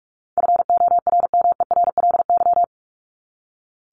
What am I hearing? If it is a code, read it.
Morse code, heard as a keyed tone, about 45 wpm, 700 Hz. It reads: FORMERLY
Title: LY Suffix - Top 100 Words with Base Words 45wpm